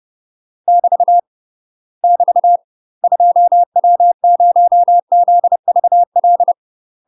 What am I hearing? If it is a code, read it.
Morse code, heard as a keyed tone, about 30 wpm, 700 Hz. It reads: <BT> <BT> 2W0ZVL
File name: UK MASTER QSOs TRAINING / 30WPM / UK MASTER QSOs 0001